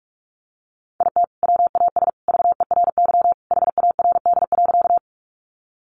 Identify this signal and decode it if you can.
Morse code, heard as a keyed tone, about 44 wpm, 700 Hz. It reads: IT WAS VERY HARD.
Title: Sentences from Top 300 Words 45wpm